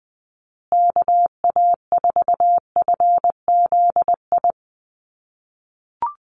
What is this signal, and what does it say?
Morse code, heard as a keyed tone, about 20 wpm, 700 Hz. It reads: KA4FZI
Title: USA Call Signs Repeated 2x 20wpm